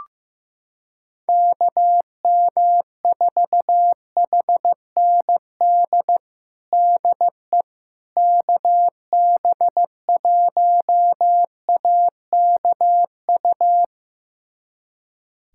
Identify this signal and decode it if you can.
Morse code, heard as a keyed tone, about 15 wpm, 700 Hz. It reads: KM4HND DE KB1AKU
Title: USA Callsign DE Callsign 15wpm